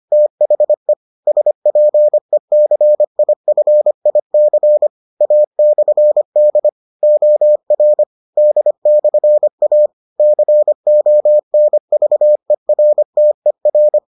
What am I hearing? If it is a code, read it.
Morse code, heard as a keyed tone, about 25 wpm, 600 Hz. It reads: THE SPECIFIC A/D OR D/A CONVERTER